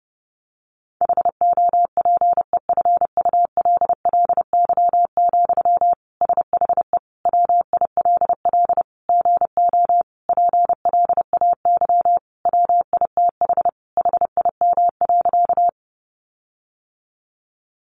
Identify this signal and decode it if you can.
Morse code, heard as a keyed tone, about 30 wpm, 700 Hz. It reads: HOPEFULLY, SHE WILL GO PLAY WITH HIM.
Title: Sentences from Top 1500 Words 30wpm